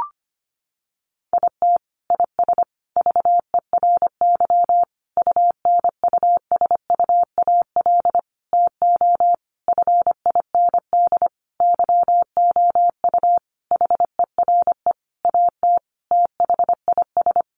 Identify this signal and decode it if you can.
Morse code, heard as a keyed tone, about 25 wpm, 700 Hz. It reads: IT IS VERY UNUSUAL TO FIND YOU HERE AT THIS
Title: Sentences from Top 1400 Words 25wpm